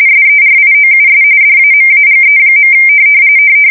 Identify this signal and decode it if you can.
RTTY, about 45.45 baud. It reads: SERIOUS DAMAGE.
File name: fsk170 / battleshipnj